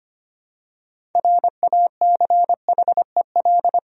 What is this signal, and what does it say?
Morse code, heard as a keyed tone, about 25 wpm, 700 Hz. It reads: RACHEL
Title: USA Common Names - 25wpm